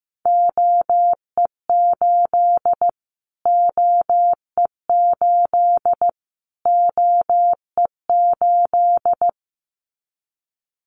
Morse code, heard as a keyed tone, about 15 wpm, 700 Hz. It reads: OE8 OE8 OE8
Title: Callsign Prefixes for Top 25 DXCC - Repeated 3x - 15wpm